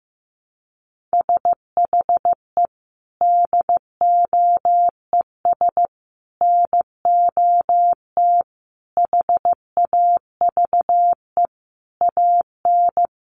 Morse code, heard as a keyed tone, about 15 wpm, 700 Hz. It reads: SHE DOES NOT HAVE AN
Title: Sentences from Top 1200 Words 15wpm